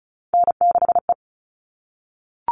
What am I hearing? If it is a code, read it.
Morse code, heard as a keyed tone, about 34 wpm, 700 Hz. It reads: N6E
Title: All Cut Numbers - 3-Digit 35wpm